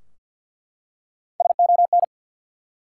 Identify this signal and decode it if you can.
Morse code, heard as a keyed tone, about 51 wpm, 700 Hz. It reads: SON
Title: Three Letter Words - 50wpm